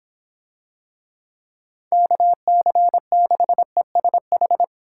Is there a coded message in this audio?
Yes: Morse code from 1.9 to 4.6 s, heard as a keyed tone, about 24 wpm, 700 Hz. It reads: KC6ESH